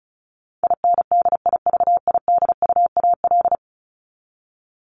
Morse code, heard as a keyed tone, about 34 wpm, 700 Hz. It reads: INDIVIDUAL
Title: ES:S Suffix - Top 100 Words with Base Words 35wpm